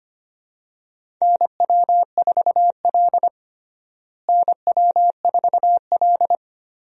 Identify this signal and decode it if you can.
Morse code, heard as a keyed tone, about 25 wpm, 700 Hz. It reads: NW4L NW4L
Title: USA Call Signs - 4 Characters - Repeated 3x - 25wpm